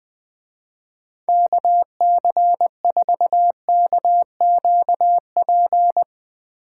Morse code, heard as a keyed tone, about 20 wpm, 700 Hz. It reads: KC4KQP
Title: Speed Racing - USA Call Signs 20wpm